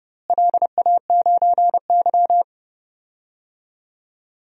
Morse code, heard as a keyed tone, about 30 wpm, 700 Hz. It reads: LA9Y
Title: DXCC-100 Call Signs - 4 Characters - Repeated 2x - 30wpm